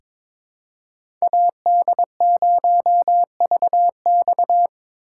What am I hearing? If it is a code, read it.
Morse code, heard as a keyed tone, about 23 wpm, 700 Hz. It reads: AD0VX
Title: USA Call Signs - 5 Characters - 4x Character Spacing - 22wpm